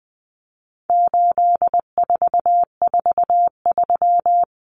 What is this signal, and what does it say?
Morse code, heard as a keyed tone, about 20 wpm, 700 Hz. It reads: 8443
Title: Common Cut Numbers - 4-Digit 20wpm